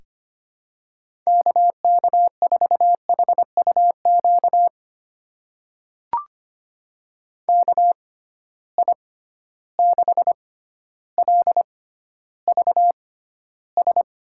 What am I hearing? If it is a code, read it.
Morse code, heard as a keyed tone, about 25 wpm, spaced as for 13 wpm, 700 Hz. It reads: KK4HUQ KI6LVS
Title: USA Call Signs - 6 Characters - 6x Character Spacing - 25wpm